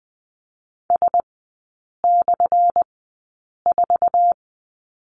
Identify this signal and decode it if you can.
Morse code, heard as a keyed tone, about 20 wpm, spaced as for 8 wpm, 700 Hz. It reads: S/4
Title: 184 Lesson 30e - Letter b - Three Characters 20wpm